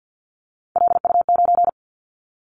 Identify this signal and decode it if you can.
Morse code, heard as a keyed tone, about 50 wpm, 700 Hz. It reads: LU9
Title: Callsign Prefixes for Top 50 DXCC - 50wpm